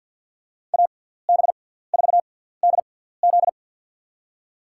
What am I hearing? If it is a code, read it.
Morse code, heard as a keyed tone, about 53 wpm, spaced as for 17 wpm, 700 Hz. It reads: AB4DZ
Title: USA Call Signs - 6x Character Spacing - 50wpm